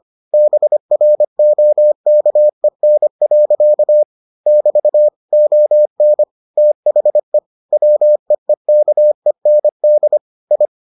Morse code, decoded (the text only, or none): BROKEN. <BT> ON THE WEEKEND I